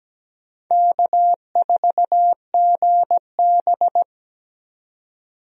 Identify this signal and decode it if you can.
Morse code, heard as a keyed tone, about 17 wpm, 700 Hz. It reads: K4GB
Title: USA Call Signs - 4 Characters - 17wpm